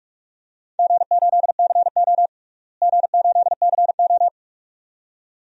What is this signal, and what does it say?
Morse code, heard as a keyed tone, about 46 wpm, 700 Hz. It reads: G8CO G8CO
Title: DXCC-100 Call Signs - 4 Characters - Repeated 2x - 45wpm